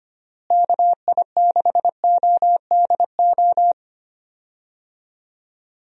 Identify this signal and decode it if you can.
Morse code, heard as a keyed tone, about 25 wpm, 700 Hz. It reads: KI6ODO
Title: USA Call Signs - 6 Characters - Repeated 3x - 25wpm